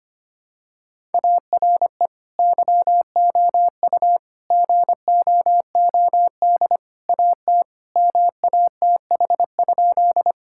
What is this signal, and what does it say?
Morse code, heard as a keyed tone, about 25 wpm, 700 Hz. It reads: ARE YOU GOOD AT MATH?
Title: Sentences from Top 1000 Words - Review Part 6 - 25wpm